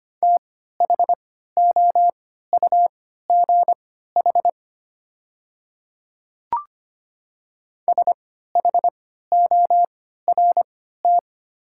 Morse code, heard as a keyed tone, about 25 wpm, spaced as for 14 wpm, 700 Hz. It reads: THOUGH SHORT
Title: 3x Character Spacing - Top 500 Words - 25wpm